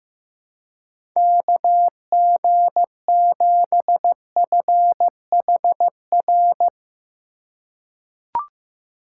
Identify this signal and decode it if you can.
Morse code, heard as a keyed tone, about 15 wpm, 700 Hz. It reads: KG7FHR
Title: USA Call Signs - 6 Characters - 4x Character Spacing - 15wpm